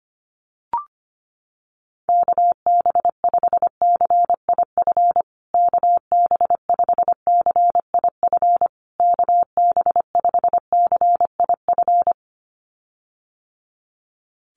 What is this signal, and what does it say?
Morse code, heard as a keyed tone, about 25 wpm, 700 Hz. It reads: KB5CIF KB5CIF KB5CIF
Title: USA Call Signs Repeated 3x 25wpm